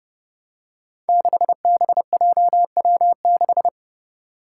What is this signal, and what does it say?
Morse code, heard as a keyed tone, about 30 wpm, 700 Hz. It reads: 6BJW6
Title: Five Challenging Letter-Number Combinations 30wpm